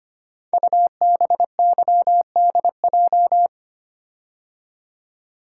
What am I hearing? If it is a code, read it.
Morse code, heard as a keyed tone, about 25 wpm, 700 Hz. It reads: UBYDJ
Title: Code Groups - 5 Letters - 25wpm